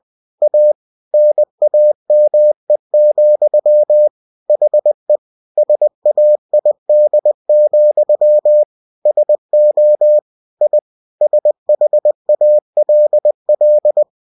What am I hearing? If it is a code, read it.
Morse code, heard as a keyed tone, about 20 wpm, 600 Hz. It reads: A NAME, HE SAID, SO I SHALL
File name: cw-books / winnie-the-pooh / winnie-20wpm-page-0095